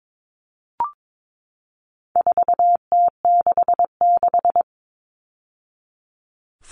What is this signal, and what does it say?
Morse code, heard as a keyed tone, about 22 wpm, 700 Hz. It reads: 4T66
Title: All Cut Numbers - 4-Digit 22wpm